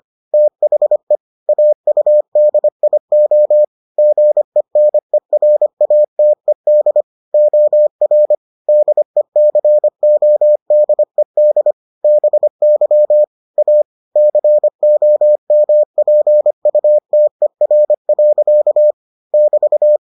Morse code, heard as a keyed tone, about 25 wpm, 600 Hz. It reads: THE AUDIO GENERATED OR DECODED BY A COMPUTER. <BT>